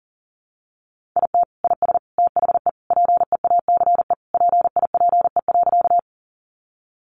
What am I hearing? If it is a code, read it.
Morse code, heard as a keyed tone, about 37 wpm, 700 Hz. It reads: IT IS THE PEACE PIPE.
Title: Sentences from Top 1500 Words 40wpm